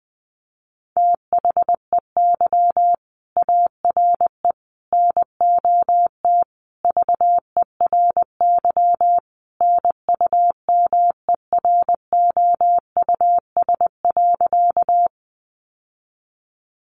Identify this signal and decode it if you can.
Morse code, heard as a keyed tone, about 20 wpm, 700 Hz. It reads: THEY ARE NOT VERY NUMEROUS.
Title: Sentences from Top 1400 Words 20wpm